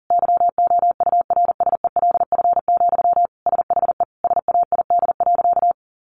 Morse code, heard as a keyed tone, about 40 wpm, 700 Hz. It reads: YOURSELF, SHE SAID.
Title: Sentences from Top 600 Words 40wpm